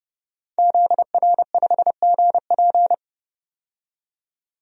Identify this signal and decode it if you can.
Morse code, heard as a keyed tone, about 32 wpm, 700 Hz. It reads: ZR5GP
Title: DXCC-100 Call Signs - 5 Characters - 30wpm